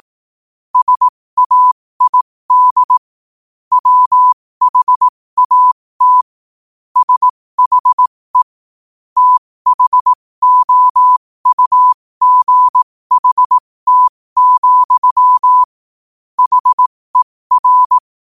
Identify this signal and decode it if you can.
Morse code, heard as a keyed tone, about 18 wpm, spaced as for 15 wpm, 1000 Hz. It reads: SAID WHAT SHE THOUGHT, HER
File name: qotdfiles / 15wpm